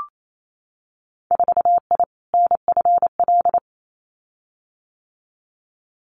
Morse code, heard as a keyed tone, about 28 wpm, 700 Hz. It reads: 4I NFL
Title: Winter Field Day Exchanges 28wpm